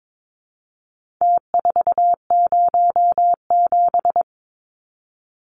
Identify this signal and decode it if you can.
Morse code, heard as a keyed tone, about 22 wpm, 700 Hz. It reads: T407
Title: Numbers - Contest Style - 4-Digit 22wpm